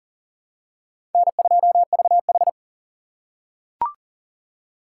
Morse code, heard as a keyed tone, about 37 wpm, 700 Hz. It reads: N2VH
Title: USA Call Signs - 4 Characters - 6x Character Spacing - 40wpm